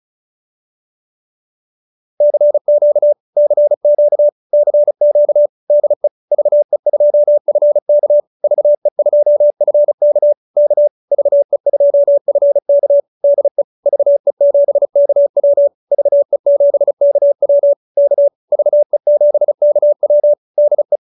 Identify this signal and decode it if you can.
Morse code, heard as a keyed tone, about 34 wpm, 590 Hz. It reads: CQ CQ CQ DE VE2FK VE2FK K VE2FK DE VE7KW VE7KW K VE7KW DE